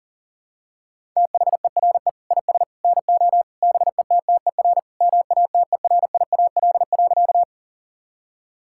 Morse code, heard as a keyed tone, about 40 wpm, 700 Hz. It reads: THERE IS NO BETTER MATERIAL.
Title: Sentences from Top 1000 Words - Review Part 5 - 40wpm